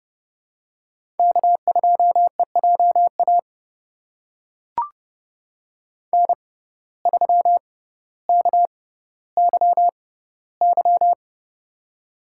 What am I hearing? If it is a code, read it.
Morse code, heard as a keyed tone, about 28 wpm, spaced as for 15 wpm, 700 Hz. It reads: K2EJA N3KYY
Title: USA Call Signs - 5 Characters - 6x Character Spacing - 30wpm